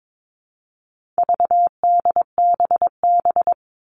Morse code, heard as a keyed tone, about 22 wpm, 700 Hz. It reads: VDBB
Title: All Cut Numbers - 4-Digit 22wpm